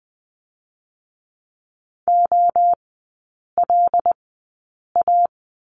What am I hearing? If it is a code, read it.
Morse code, heard as a keyed tone, about 20 wpm, spaced as for 8 wpm, 700 Hz. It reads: OLA
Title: 121 Lesson 20e - Number 3 - Three Characters 20wpm